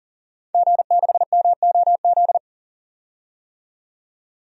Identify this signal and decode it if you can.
Morse code, heard as a keyed tone, about 40 wpm, 700 Hz. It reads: G6MOZ
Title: DXCC-100 Call Signs - 5 Characters - 40wpm